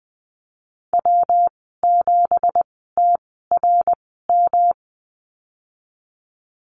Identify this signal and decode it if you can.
Morse code, heard as a keyed tone, about 20 wpm, spaced as for 15 wpm, 700 Hz. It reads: W7TRM
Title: 173 Lesson 28g - Number 9 - Call Signs 20wpm